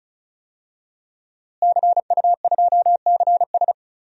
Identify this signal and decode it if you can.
Morse code, heard as a keyed tone, about 34 wpm, 700 Hz. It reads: CU2CS
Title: DXCC-100 Call Signs - 5 Characters - Repeated 3x - 35wpm